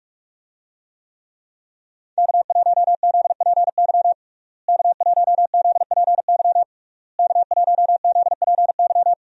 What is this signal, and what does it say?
Morse code, heard as a keyed tone, about 42 wpm, 700 Hz. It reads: K1ZPY K1ZPY K1ZPY
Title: USA Call Signs - 5 Characters - Repeated 3x - 45wpm